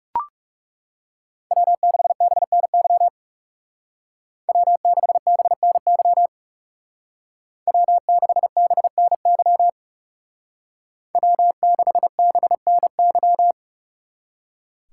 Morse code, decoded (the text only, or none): W6BNY W6BNY W6BNY W6BNY